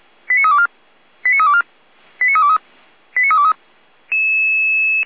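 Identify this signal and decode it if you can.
Five-tone selective call (ZVEI), heard as quick runs of five tones, tones of about 70 ms. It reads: 89324 89324 89323 89323
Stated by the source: ZVEI five-tone call 89503 received by SDR.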